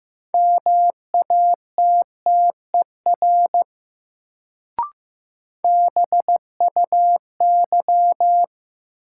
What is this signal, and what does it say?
Morse code, heard as a keyed tone, about 15 wpm, 700 Hz. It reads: MATTER BUY
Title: Sentences from Top 700 Words - New Word Drill - 15wpm